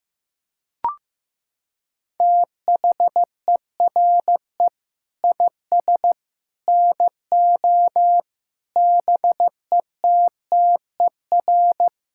Morse code, heard as a keyed tone, about 16 wpm, 690 Hz. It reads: THERE IS NO BETTER
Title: Sentences from Top 1000 Words - Review Part 5 - 15wpm